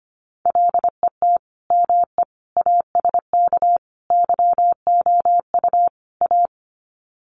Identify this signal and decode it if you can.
Morse code, heard as a keyed tone, about 25 wpm, 700 Hz. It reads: LET ME ASK YOU A
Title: Sets of 5 Words Encore 25wpm